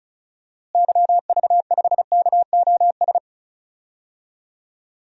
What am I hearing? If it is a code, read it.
Morse code, heard as a keyed tone, about 36 wpm, 700 Hz. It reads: YV5KOS
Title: DXCC-100 Call Signs - 6 Characters - Repeated 2x - 35wpm